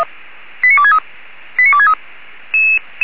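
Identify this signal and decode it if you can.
Five-tone selective call (ZVEI), heard as quick runs of five tones, tones of about 80 ms. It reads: 89263 89263
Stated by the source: ZVEI five-tone call received by SDR.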